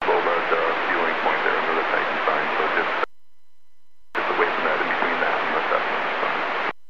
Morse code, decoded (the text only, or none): none